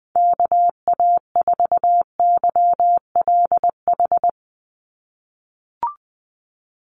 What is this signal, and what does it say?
Morse code, heard as a keyed tone, about 20 wpm, 700 Hz. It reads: KA4YLH
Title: USA Call Signs 20wpm